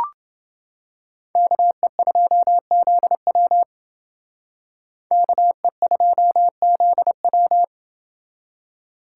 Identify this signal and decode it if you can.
Morse code, heard as a keyed tone, about 28 wpm, 700 Hz. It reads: KE2ZW KE2ZW